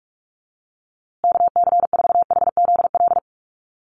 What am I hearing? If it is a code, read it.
Morse code, heard as a keyed tone, about 44 wpm, 700 Hz. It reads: KC4HZL
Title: USA Call Signs Repeated 3x 45wpm